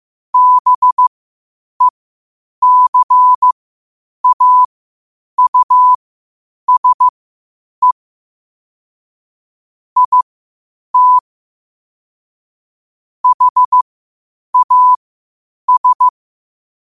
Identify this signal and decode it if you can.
Morse code, heard as a keyed tone, about 15 wpm, spaced as for 8 wpm, 1000 Hz. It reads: BECAUSE IT HAS
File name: qotdfiles / 8wpm